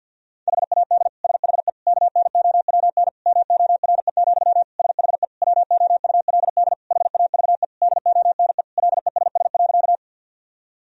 Morse code, heard as a keyed tone, about 51 wpm, 700 Hz. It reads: HAD SHE KNOWN MORE, SHE WOULD HAVE DONE LESS.
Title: Sentences from Top 1000 Words - Review Part 3 - 50wpm